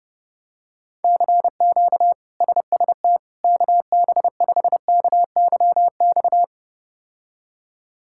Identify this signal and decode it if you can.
Morse code, heard as a keyed tone, about 30 wpm, 700 Hz. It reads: CQ SST KB5KYX
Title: K1USN SST QSO Practice - 30wpm